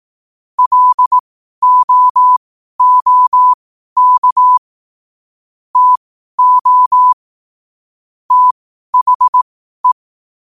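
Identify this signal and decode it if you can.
Morse code, heard as a keyed tone, about 18 wpm, spaced as for 12 wpm, 1000 Hz. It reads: LOOK TO THE
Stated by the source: Morse code at 12 wpm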